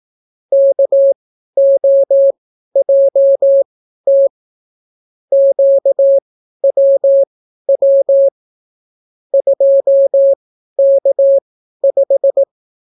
Morse code, heard as a keyed tone, about 18 wpm, spaced as for 12 wpm, 550 Hz. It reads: KOJT QWW 2K5